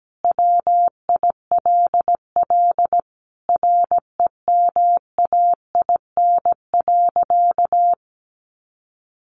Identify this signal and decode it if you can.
Morse code, heard as a keyed tone, about 17 wpm, 700 Hz. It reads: WILL REMAIN.
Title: Sentences from Top 1200 Words 17wpm